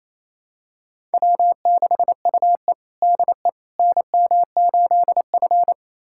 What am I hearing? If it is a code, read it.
Morse code, heard as a keyed tone, about 28 wpm, 700 Hz. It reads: W6UE DE NM8F
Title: USA Callsign DE Callsign - 4 Characters - 28wpm